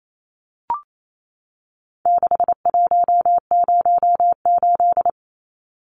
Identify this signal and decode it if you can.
Morse code, heard as a keyed tone, about 28 wpm, 700 Hz. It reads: 6108
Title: Numbers - 4-Digit 28wpm